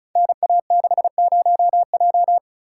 Morse code, heard as a keyed tone, about 36 wpm, 700 Hz. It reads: NA60J
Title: Code Groups - 5 Letter:Numbers - 35wpm